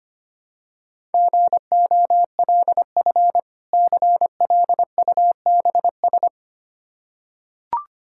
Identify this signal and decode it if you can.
Morse code, heard as a keyed tone, about 25 wpm, 700 Hz. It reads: GOLF CLUBS
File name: practice / 25wpm / Difficult Words with Hints - 25wpm